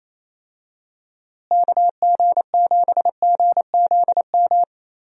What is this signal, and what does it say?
Morse code, heard as a keyed tone, about 28 wpm, 700 Hz. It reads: KG7GZM
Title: USA Call Signs - 6 Characters - Repeated 2x - 28wpm